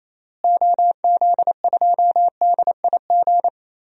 Morse code, heard as a keyed tone, about 28 wpm, 700 Hz. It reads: OZ2DIG
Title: DXCC-100 Call Signs - 6 Characters - Repeated 2x - 28wpm